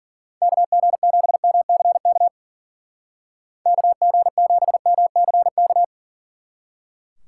Morse code, heard as a keyed tone, about 49 wpm, 700 Hz. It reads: KG7MCK KG7MCK
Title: Speed Racing - USA 2x3 Call Signs 40wpm